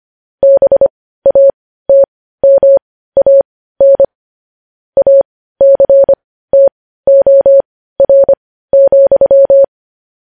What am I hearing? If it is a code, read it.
Morse code, heard as a keyed tone, about 25 wpm, spaced as for 15 wpm, 550 Hz. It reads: BATMAN ACTOR,
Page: news headlines in morse code at 15 wpm